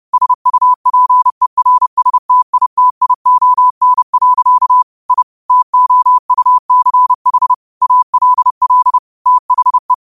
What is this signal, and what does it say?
Morse code, heard as a keyed tone, about 30 wpm, 1000 Hz. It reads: SUPERSTITION. I TOUCH ALL THE
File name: qotdfiles / 30wpm